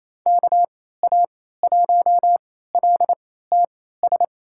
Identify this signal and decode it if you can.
Morse code, heard as a keyed tone, about 28 wpm, spaced as for 16 wpm, 700 Hz. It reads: KA1LTS
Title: USA Call Signs - 3x Character Spacing - 28wpm